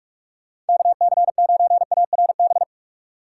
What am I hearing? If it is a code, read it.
Morse code, heard as a keyed tone, about 59 wpm, 700 Hz. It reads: KC9ARB